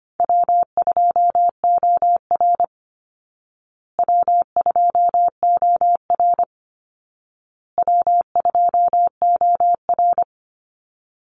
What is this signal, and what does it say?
Morse code, heard as a keyed tone, about 25 wpm, 700 Hz. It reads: W2OR W2OR W2OR
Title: Easy Call Signs Repeated 3x - 25wpm